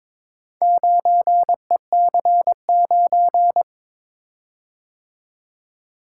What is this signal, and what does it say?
Morse code, heard as a keyed tone, about 22 wpm, 700 Hz. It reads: 9EC9
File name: practice / 22wpm / Four Letter-Number Combinations - 22wpm